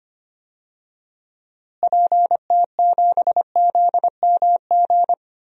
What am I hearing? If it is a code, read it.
Morse code, heard as a keyed tone, about 25 wpm, 700 Hz. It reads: PT7ZMG
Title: DXCC-100 Call Signs - 6 Characters - Repeated 2x - 25wpm